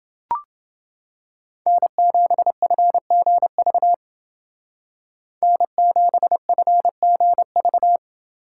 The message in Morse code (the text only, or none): N7FGV N7FGV